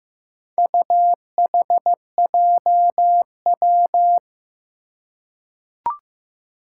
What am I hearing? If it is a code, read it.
Morse code, heard as a keyed tone, about 15 wpm, 700 Hz. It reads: UHJW
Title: Four Challenging Letter-Number Combinations 15wpm